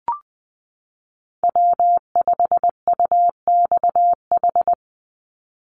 Morse code, heard as a keyed tone, about 20 wpm, 700 Hz. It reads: W5UXH
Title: Call Signs Repeated 2x - 20wpm